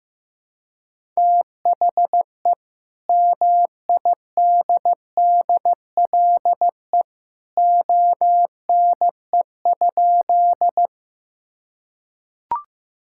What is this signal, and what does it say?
Morse code, heard as a keyed tone, about 15 wpm, 700 Hz. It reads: THE MIDDLE ONE?
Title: Sentences from Top 1000 Words - Review Part 6 - 15wpm